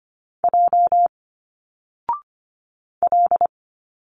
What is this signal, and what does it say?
Morse code, heard as a keyed tone, about 25 wpm, 700 Hz. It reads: J L
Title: Single Letters 25wpm